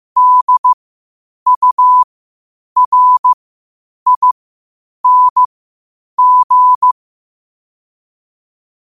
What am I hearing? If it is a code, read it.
Morse code, heard as a keyed tone, about 15 wpm, spaced as for 8 wpm, 1000 Hz. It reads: DURING
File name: qotdfiles / 8wpm